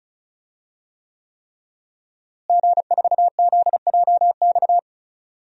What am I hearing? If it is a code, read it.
Morse code, heard as a keyed tone, about 34 wpm, 700 Hz. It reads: G4ZJX
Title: DXCC-100 Call Signs - 5 Characters - Repeated 3x - 35wpm